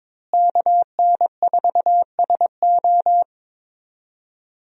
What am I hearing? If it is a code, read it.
Morse code, heard as a keyed tone, about 21 wpm, 700 Hz. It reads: KN4SO
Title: USA Call Signs - 5 Characters - Repeated 3x - 22wpm